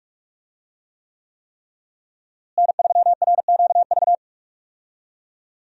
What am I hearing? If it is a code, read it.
Morse code, heard as a keyed tone, about 46 wpm, 700 Hz. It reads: N3RQV
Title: USA Call Signs - 5 Characters - Repeated 3x - 45wpm